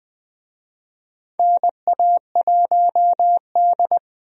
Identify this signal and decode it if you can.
Morse code, heard as a keyed tone, about 20 wpm, 700 Hz. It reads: NA1D
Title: USA Call Signs - 4 Characters - Repeated 3x - 20wpm